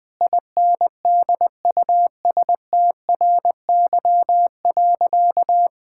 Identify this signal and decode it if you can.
Morse code, heard as a keyed tone, about 20 wpm, 700 Hz. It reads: INDUSTRY.
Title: Sentences with 8-Word Phrases - Spoken Beginning - 20wpm